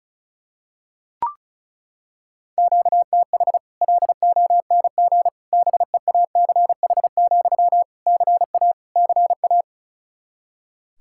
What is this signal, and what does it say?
Morse code, heard as a keyed tone, about 36 wpm, 700 Hz. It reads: QTH LONG BEACH, CA CA
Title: Practice QSO - Exchange 1 - Beginning - 35wpm